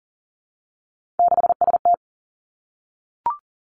Morse code, heard as a keyed tone, about 42 wpm, 700 Hz. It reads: 6ST
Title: Three Challenging Letter-Number Combinations 40wpm